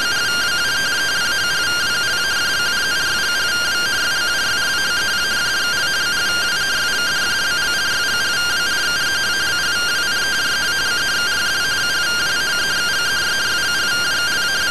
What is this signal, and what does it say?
Two-tone teleprinter signal, about 170 Hz shift.